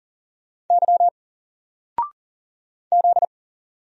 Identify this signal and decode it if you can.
Morse code, heard as a keyed tone, about 40 wpm, 700 Hz. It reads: Y Z
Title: Single Letter-Number 40wpm